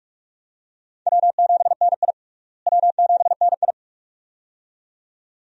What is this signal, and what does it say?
Morse code, heard as a keyed tone, about 44 wpm, 700 Hz. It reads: W7NI W7NI
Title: USA Call Signs - 4 Characters - Repeated 2x - 45wpm